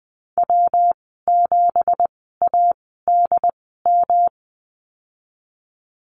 Morse code, heard as a keyed tone, about 20 wpm, spaced as for 15 wpm, 700 Hz. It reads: W7ADM
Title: 186 Lesson 30g - Letter b - Call Signs 20wpm